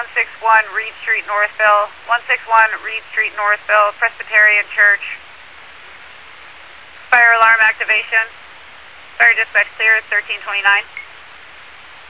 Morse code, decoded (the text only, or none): none